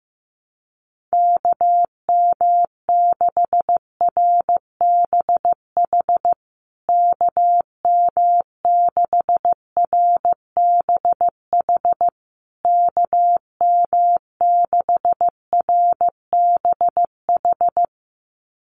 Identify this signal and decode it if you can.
Morse code, heard as a keyed tone, about 15 wpm, 700 Hz. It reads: KM6RBH KM6RBH KM6RBH
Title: USA Call Signs Repeated 3x 15wpm